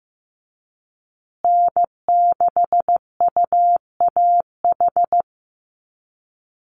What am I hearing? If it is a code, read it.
Morse code, heard as a keyed tone, about 15 wpm, 700 Hz. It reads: N6UAH
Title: USA Call Signs Repeated 2x 15wpm